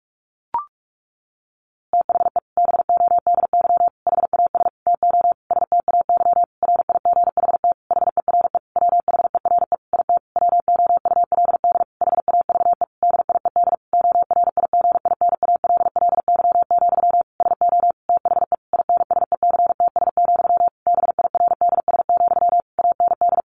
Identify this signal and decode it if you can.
Morse code, heard as a keyed tone, about 44 wpm, 700 Hz. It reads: THE BODY HAS TO STAY RIGHT HERE WHERE IT WOULD HAVE DIED ORIGINALLY, SO THE INSECTS, BIRDS, AND
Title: A Sound Of Thunder 45wpm